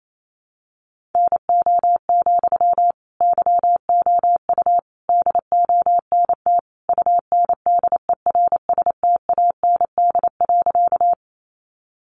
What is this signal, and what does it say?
Morse code, heard as a keyed tone, about 28 wpm, 700 Hz. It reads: NO, YOU DONT UNDERSTAND.
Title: Sentences from Top 600 Words 28wpm